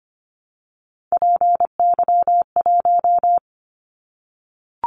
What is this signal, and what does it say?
Morse code, heard as a keyed tone, about 25 wpm, 700 Hz. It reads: PY1
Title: Callsign Prefixes for Top 100 DXCC - 25wpm